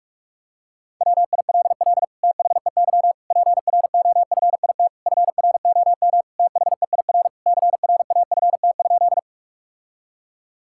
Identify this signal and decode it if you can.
Morse code, heard as a keyed tone, about 46 wpm, 700 Hz. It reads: WILL THEY PROFIT FROM THEIR CRAFT?
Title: Sentences from Top 1000 Words - Review Part 6 - 45wpm